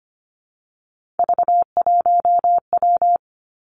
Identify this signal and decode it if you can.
Morse code, heard as a keyed tone, about 25 wpm, 700 Hz. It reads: V1W
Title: Three Challenging Letter-Number Combinations 25wpm